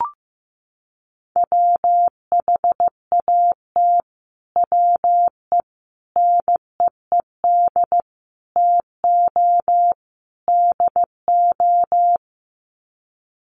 Morse code, heard as a keyed tone, about 15 wpm, 700 Hz. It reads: WHAT WE NEED TO DO
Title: Sets of 5 Words Encore 15wpm